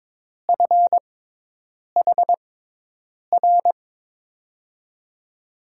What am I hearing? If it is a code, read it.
Morse code, heard as a keyed tone, about 22 wpm, 700 Hz. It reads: F H R
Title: USA Call Signs - 6 Characters - 6x Character Spacing - 22wpm